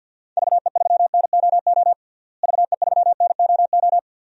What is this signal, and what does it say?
Morse code, heard as a keyed tone, about 50 wpm, 700 Hz. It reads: VE3NOO VE3NOO
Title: DXCC-100 Call Signs - 6 Characters - Repeated 2x - 50wpm